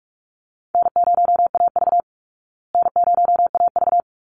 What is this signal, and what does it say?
Morse code, heard as a keyed tone, about 44 wpm, 700 Hz. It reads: N0AV N0AV
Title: Easy Call Signs Repeated 2x - 45wpm